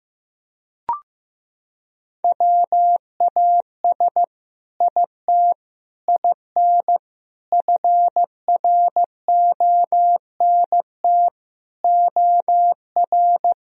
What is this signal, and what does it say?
Morse code, heard as a keyed tone, about 15 wpm, 700 Hz. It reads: WAS IT IN FRONT OR
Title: Sentences from Top 1000 Words - Review Part 2 - 15wpm